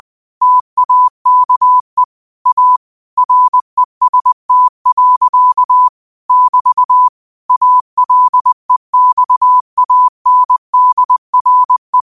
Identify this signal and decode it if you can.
Morse code, heard as a keyed tone, about 20 wpm, 1000 Hz. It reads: TAKE A REST. <BT> ALEXANDRE